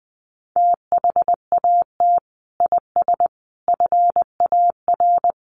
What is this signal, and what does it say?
Morse code, heard as a keyed tone, about 20 wpm, 700 Hz. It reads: THAT IS FAR
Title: Sentences from Top 1200 Words 20wpm